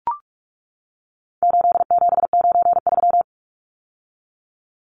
Morse code, heard as a keyed tone, about 44 wpm, 700 Hz. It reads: 8793
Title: Numbers - 4-Digit 45wpm